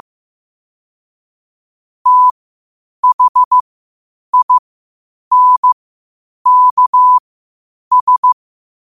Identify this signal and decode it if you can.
Morse code, heard as a keyed tone, about 15 wpm, spaced as for 8 wpm, 1000 Hz. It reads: THINKS
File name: qotdfiles / 8wpm